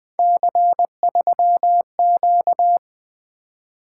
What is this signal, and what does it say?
Morse code, heard as a keyed tone, about 20 wpm, 700 Hz. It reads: C3Q
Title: Three Letter-Number Combinations - 20wpm